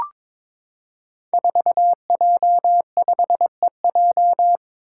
Morse code, heard as a keyed tone, about 22 wpm, 700 Hz. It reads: 4J5EJ
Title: Five Challenging Letter-Number Combinations 22wpm